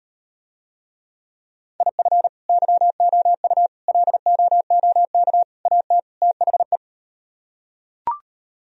Morse code, heard as a keyed tone, about 38 wpm, 700 Hz. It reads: IF YOU LOOK AT THE